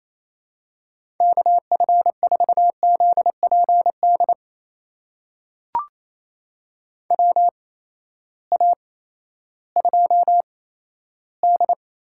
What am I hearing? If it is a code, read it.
Morse code, heard as a keyed tone, about 28 wpm, spaced as for 14 wpm, 700 Hz. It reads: KF4ZPD WA2D
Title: USA Call Signs - 6 Characters - 8x Character Spacing - 28wpm